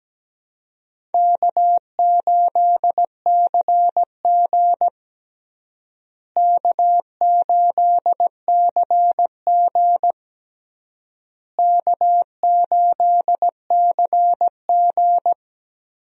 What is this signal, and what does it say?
Morse code, heard as a keyed tone, about 17 wpm, 700 Hz. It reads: K8CG K8CG K8CG
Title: USA Call Signs - 4 Characters - Repeated 3x - 17wpm